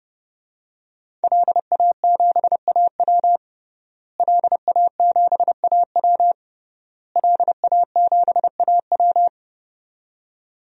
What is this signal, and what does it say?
Morse code, heard as a keyed tone, about 30 wpm, 700 Hz. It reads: LA7AW LA7AW LA7AW
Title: DXCC-100 Call Signs - 5 Characters - Repeated 3x - 30wpm